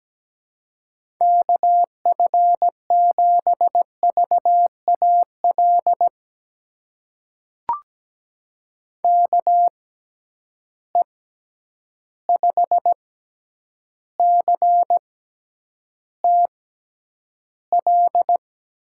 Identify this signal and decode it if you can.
Morse code, heard as a keyed tone, about 17 wpm, spaced as for 9 wpm, 700 Hz. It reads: KF7VAL KE5CTL
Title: USA Call Signs - 6 Characters - 6x Character Spacing - 17wpm